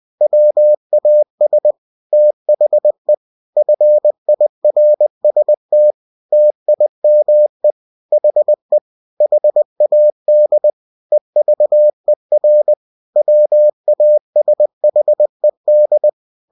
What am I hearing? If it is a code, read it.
Morse code, heard as a keyed tone, about 20 wpm, 600 Hz. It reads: WAS THE FIRST TIME HE HAD EVER WASHED
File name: cw-books / winnie-the-pooh / winnie-20wpm-page-0085